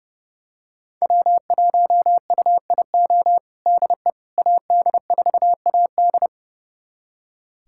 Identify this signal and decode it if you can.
Morse code, heard as a keyed tone, about 32 wpm, 700 Hz. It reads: W1UIO DE AD4AD
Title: USA Callsign DE Callsign - 5 Characters - 30wpm